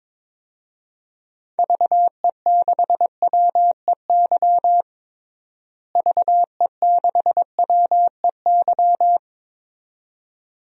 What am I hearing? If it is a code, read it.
Morse code, heard as a keyed tone, about 21 wpm, 700 Hz. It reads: VE6WEY VE6WEY
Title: DXCC-100 Call Signs - 6 Characters - Repeated 2x - 22wpm